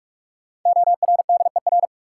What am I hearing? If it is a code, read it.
Morse code, heard as a keyed tone, about 44 wpm, 700 Hz. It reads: ORDER
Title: Sentences from Top 400 Words - New Word Drill - 45wpm